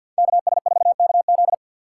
Morse code, heard as a keyed tone, about 53 wpm, 700 Hz. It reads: KS4KZ
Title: USA Call Signs - 6x Character Spacing - 50wpm